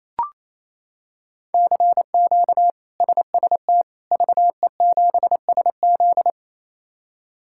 Morse code, heard as a keyed tone, about 27 wpm, 700 Hz. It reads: CQ SST VE7SZ
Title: K1USN SST QSO Practice - 28wpm